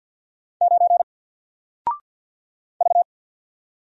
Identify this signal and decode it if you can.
Morse code, heard as a keyed tone, about 48 wpm, 700 Hz. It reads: 9 V
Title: Single Character (Letter:Number:Punctuation:Prosign) Alt 50wpm